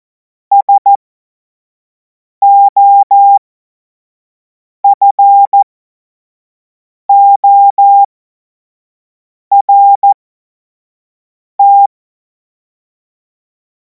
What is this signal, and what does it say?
Morse code, heard as a keyed tone, about 14 wpm, spaced as for 5 wpm, 800 Hz. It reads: SOFORT
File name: Morsekurs / Witze / Witz 03. Februar 2026 25 F70